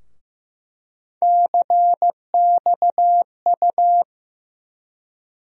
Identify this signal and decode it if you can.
Morse code, heard as a keyed tone, about 15 wpm, 700 Hz. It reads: CXU
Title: USA Call Sign Postfixes - 3 Letters 15wpm